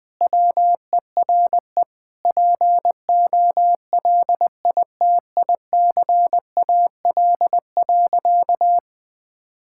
Morse code, heard as a keyed tone, about 20 wpm, 700 Hz. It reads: WERE POLITICAL.
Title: Sentences from Top 1000 Words - Review Part 5 - 20wpm